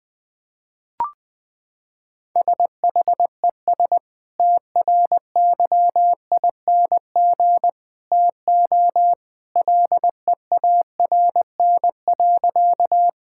Morse code, heard as a keyed tone, about 20 wpm, 700 Hz. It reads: SHES TRYING TO LEARN.
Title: Sentences with 2-Word Phrases - 20wpm